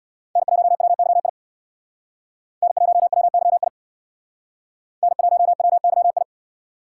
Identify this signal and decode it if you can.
Morse code, heard as a keyed tone, about 62 wpm, 700 Hz. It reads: N1WYI N1WYI N1WYI
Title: Speed Racing - USA Call Signs 50wpm